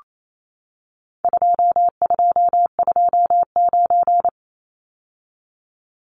Morse code, heard as a keyed tone, about 28 wpm, 700 Hz. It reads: 2229
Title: Numbers - 4-Digit 28wpm